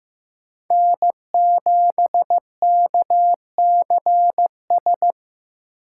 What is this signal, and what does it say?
Morse code, heard as a keyed tone, about 15 wpm, 700 Hz. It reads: N7KCS
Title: USA Call Signs - 3x Character Spacing - 15wpm